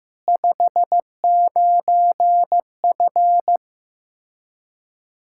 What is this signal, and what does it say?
Morse code, heard as a keyed tone, about 15 wpm, 700 Hz. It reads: 59F
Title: Practice QSO - Exchange 2 - Info - 15wpm